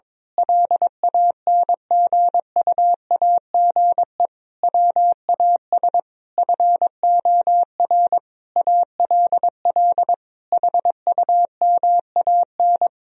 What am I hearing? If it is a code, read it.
Morse code, heard as a keyed tone, about 22 wpm, 700 Hz. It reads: LANGUAGE WAS FOR ALL HUMAN
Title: Sentences from Top 1000 Words - Review Part 4 - 22wpm